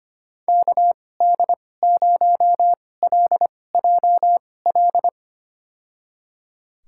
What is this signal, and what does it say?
Morse code, heard as a keyed tone, about 25 wpm, spaced as for 18 wpm, 700 Hz. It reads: KD0LJL
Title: USA Call Signs - 6 Characters - 2x Character Spacing - 25wpm